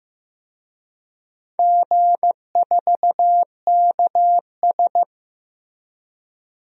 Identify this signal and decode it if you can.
Morse code, heard as a keyed tone, about 15 wpm, 700 Hz. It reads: G4KS
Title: DXCC-100 Call Signs - 4 Characters - Repeated 2x - 15wpm